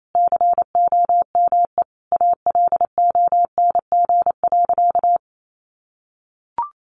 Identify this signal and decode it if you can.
Morse code, heard as a keyed tone, about 28 wpm, 700 Hz. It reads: COME ALONG.
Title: Sentences from Top 700 Words 28wpm